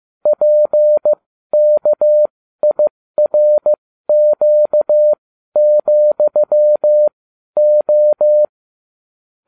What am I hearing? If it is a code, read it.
Morse code, heard as a keyed tone, about 15 wpm, spaced as for 12 wpm, 600 Hz. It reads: PKIRQ,O